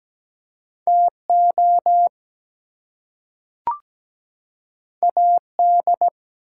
Morse code, heard as a keyed tone, about 17 wpm, 700 Hz. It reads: TO AD
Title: Two Letter Words - 17wpm